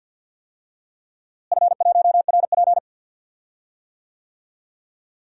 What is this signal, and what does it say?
Morse code, heard as a keyed tone, about 52 wpm, 700 Hz. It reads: F1RP
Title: DXCC-100 Call Signs - 4 Characters - Repeated 3x - 50wpm